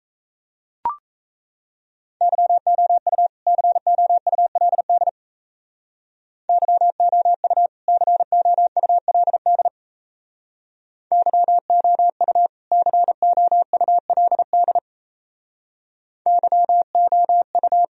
Morse code, heard as a keyed tone, about 36 wpm, 700 Hz. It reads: YOU COULD YOU COULD YOU COULD YOU